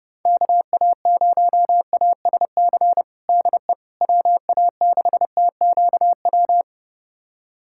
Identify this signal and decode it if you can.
Morse code, heard as a keyed tone, about 30 wpm, 700 Hz. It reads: KA0ASC DE WA6TQW
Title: USA Callsign DE Callsign - 6 Characters - 30wpm